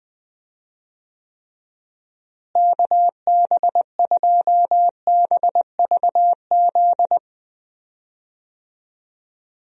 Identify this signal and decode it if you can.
Morse code, heard as a keyed tone, about 20 wpm, 700 Hz. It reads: KB2BVZ
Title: USA Call Signs - 6 Characters - Repeated 2x - 20wpm